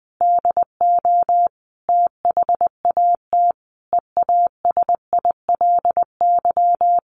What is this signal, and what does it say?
Morse code, heard as a keyed tone, about 20 wpm, 700 Hz. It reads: DO THAT EASILY
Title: Sentences from Top 900 Words 20wpm